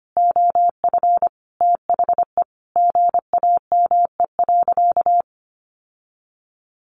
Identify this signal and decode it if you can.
Morse code, heard as a keyed tone, about 25 wpm, 700 Hz. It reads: OF THE GAME.
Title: Sentences from Top 1400 Words 25wpm